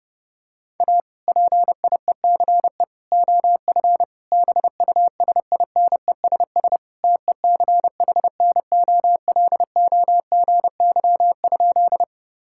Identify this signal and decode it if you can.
Morse code, heard as a keyed tone, about 30 wpm, 700 Hz. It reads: A PIECE OF BUSINESS TECHNOLOGY?
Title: Sentences from Top 1000 Words - Review Part 5 - 30wpm